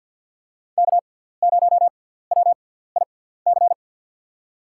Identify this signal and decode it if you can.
Morse code, heard as a keyed tone, about 47 wpm, spaced as for 17 wpm, 700 Hz. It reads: K0WIC
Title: USA Call Signs - 5 Characters - 6x Character Spacing - 50wpm